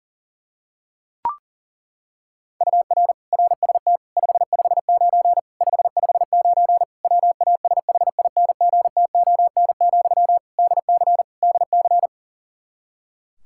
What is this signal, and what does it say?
Morse code, heard as a keyed tone, about 37 wpm, 700 Hz. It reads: UR RST 559 559 WASHINGTON, DC DC
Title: Practice QSO - Exchange 1 - Beginning - 40wpm